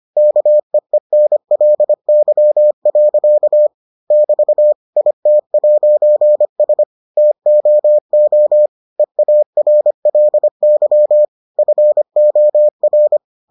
Morse code, heard as a keyed tone, about 25 wpm, 600 Hz. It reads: KEENLY. <BT> IT'S TOO EARLY FOR